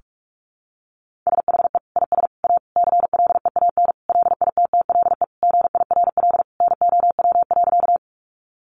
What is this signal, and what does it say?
Morse code, heard as a keyed tone, about 44 wpm, 700 Hz. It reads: SHE IS A CLEAN LITTLE GIRL NOW.
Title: Sentences from Top 1100 Words 45wpm